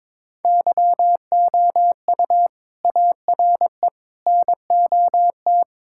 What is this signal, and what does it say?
Morse code, heard as a keyed tone, about 21 wpm, 700 Hz. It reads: YOU ARE NOT
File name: Sentences with 7-Word Phrases - 22wpm